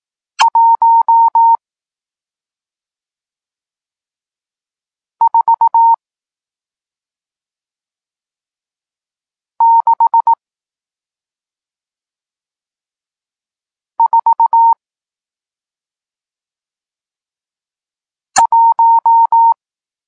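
Morse code, heard as a keyed tone, about 18 wpm, spaced as for 2 wpm, 920 Hz. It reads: 14641
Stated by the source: Morse code lesson teaching digits 1, 4 and 6, from SP9MOA, Niepołomicki Klub Krótkofalowców (Poland) amateur radio club.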